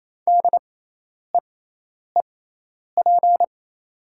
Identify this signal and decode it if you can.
Morse code, heard as a keyed tone, about 28 wpm, spaced as for 10 wpm, 700 Hz. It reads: DEEP
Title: 6x Character Spacing - Top 500 Words - 28wpm